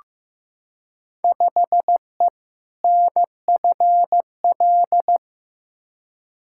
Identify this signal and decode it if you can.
Morse code, heard as a keyed tone, about 15 wpm, 700 Hz. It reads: 5E NFL
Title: Field Day Exchanges Encore - 15wpm